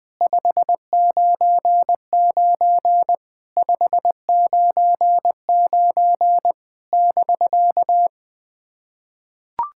Morse code, heard as a keyed tone, about 20 wpm, 700 Hz. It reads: 599 599 <BK>
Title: POTA QSO Practice - 20wpm